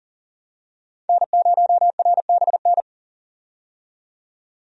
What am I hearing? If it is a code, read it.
Morse code, heard as a keyed tone, about 40 wpm, 700 Hz. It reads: N0RBN
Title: USA Call Signs - 5 Characters - 40wpm